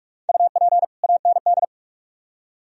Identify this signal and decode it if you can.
Morse code, heard as a keyed tone, about 42 wpm, 700 Hz. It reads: UP AND